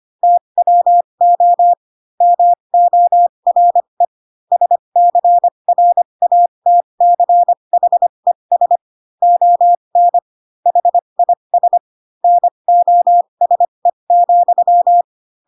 Morse code, heard as a keyed tone, about 25 wpm, spaced as for 22 wpm, 700 Hz. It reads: TWO MORE SCRATCHES ON HIS NOSE,
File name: Pinocchio / Pinocchio ch2 18 25 ik1hge